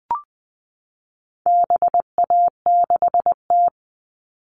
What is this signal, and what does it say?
Morse code, heard as a keyed tone, about 20 wpm, 700 Hz. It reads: BA6T
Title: All Cut Numbers - 4-Digit 20wpm